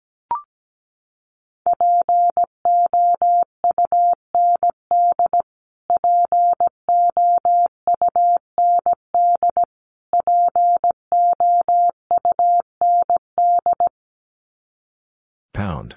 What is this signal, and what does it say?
Morse code, heard as a keyed tone, about 17 wpm, 700 Hz. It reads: POUND POUND POUND
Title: Top 500 Words Repeated 17wpm